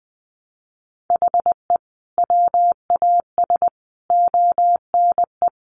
Morse code, heard as a keyed tone, about 20 wpm, 700 Hz. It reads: HE WAS ONE
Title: Sentences from Top 1300 Words 20wpm